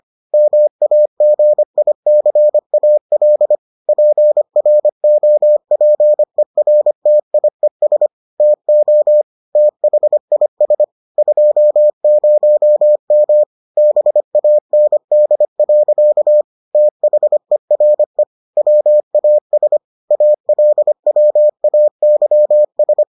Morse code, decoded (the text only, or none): MAGICAL PROPERTIES TO THIS 20M BAND. THERE WAS ALWAYS